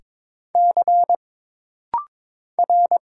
Morse code, heard as a keyed tone, about 22 wpm, 700 Hz. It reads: C R
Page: This rapid-fire timing is geared to push your ICR (Instant Character Recognition) skills.